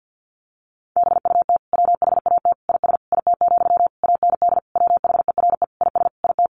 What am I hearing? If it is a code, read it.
Morse code, heard as a keyed tone, about 50 wpm, 700 Hz. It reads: BUT WHAT IS IT, AND WHERE IS IT
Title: Sentences from Top 300 Words 50wpm